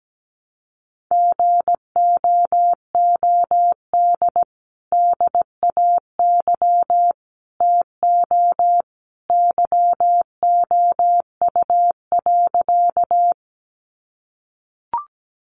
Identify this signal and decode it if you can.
Morse code, heard as a keyed tone, about 17 wpm, 700 Hz. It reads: GOOD DAY TO YOU.
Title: Sentences from Top 200 Words 17wpm